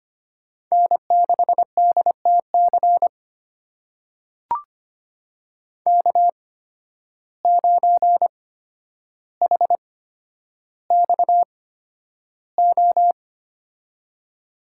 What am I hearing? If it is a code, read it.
Morse code, heard as a keyed tone, about 25 wpm, spaced as for 11 wpm, 700 Hz. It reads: N6DTC K9HXO
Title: USA Call Signs - 5 Characters - 8x Character Spacing - 25wpm